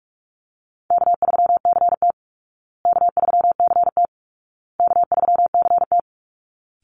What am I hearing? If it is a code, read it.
Morse code, heard as a keyed tone, about 44 wpm, 700 Hz. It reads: K3CT K3CT K3CT
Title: Call Signs Repeated 2x - 45wpm